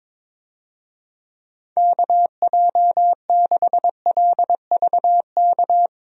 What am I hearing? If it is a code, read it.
Morse code, heard as a keyed tone, about 22 wpm, 700 Hz. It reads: KJ6LVK
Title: USA Call Signs - 6 Characters - Repeated 3x - 22wpm